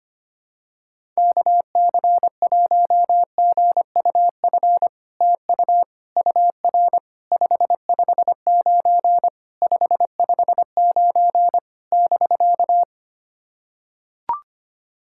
Morse code, heard as a keyed tone, about 25 wpm, 700 Hz. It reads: KC1GUF TU UR 559 559 <BK>
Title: SOTA QSO Practice 25wpm